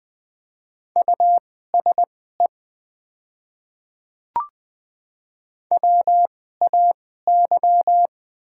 Morse code, heard as a keyed tone, about 21 wpm, spaced as for 15 wpm, 700 Hz. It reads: USE WAY
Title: 2x Character Spacing - Top 100 Words - 20wpm